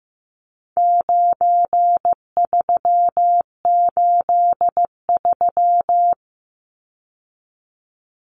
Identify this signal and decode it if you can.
Morse code, heard as a keyed tone, about 15 wpm, 700 Hz. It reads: 9383
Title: Numbers - 4-Digit 15wpm